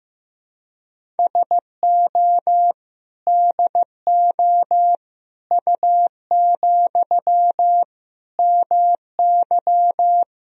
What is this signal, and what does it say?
Morse code, heard as a keyed tone, about 15 wpm, 700 Hz. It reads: SO DO U, MY
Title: Sentences from Top 400 Words and Abbreviations 15wpm